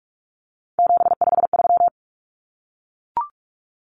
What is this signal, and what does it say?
Morse code, heard as a keyed tone, about 44 wpm, 700 Hz. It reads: 753
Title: Common Cut Numbers - 3-Digit 45wpm